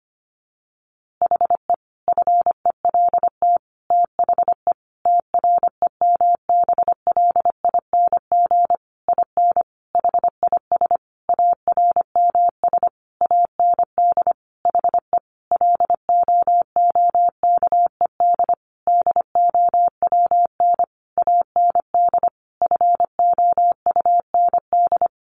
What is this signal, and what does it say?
Morse code, heard as a keyed tone, about 25 wpm, 700 Hz. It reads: HE FELT THE TREMBLING IN HIS ARMS AND HE LOOKED DOWN AND FOUND
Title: A Sound Of Thunder 25wpm